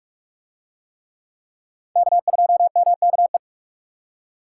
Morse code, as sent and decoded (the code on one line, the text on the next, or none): -.- ..--- -- -.- .
K2MKE